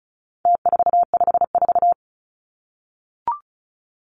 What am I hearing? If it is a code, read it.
Morse code, heard as a keyed tone, about 34 wpm, 700 Hz. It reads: T454
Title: Numbers - Contest Style - 4-Digit 35wpm